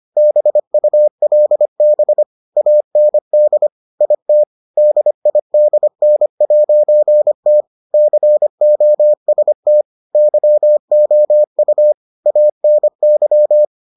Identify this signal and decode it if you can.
Morse code, heard as a keyed tone, about 25 wpm, 600 Hz. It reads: BULB AND IT DIDN'T COST YOU ANY